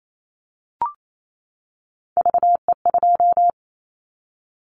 Morse code, heard as a keyed tone, about 28 wpm, 700 Hz. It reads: VE2
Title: Callsign Prefixes for Top 100 DXCC - 28wpm